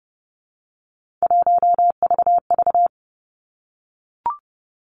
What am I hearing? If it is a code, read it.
Morse code, heard as a keyed tone, about 30 wpm, 700 Hz. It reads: 1VV
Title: Three Challenging Letter-Number Combinations 30wpm